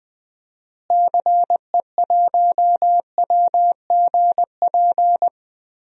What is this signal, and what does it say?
Morse code, heard as a keyed tone, about 21 wpm, 700 Hz. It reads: CE1WGP
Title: DXCC-100 Call Signs - 6 Characters - Repeated 3x - 20wpm